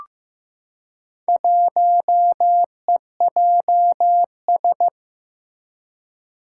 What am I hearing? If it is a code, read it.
Morse code, heard as a keyed tone, about 15 wpm, 700 Hz. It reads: 1EJS
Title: Four Challenging Letter-Number Combinations 15wpm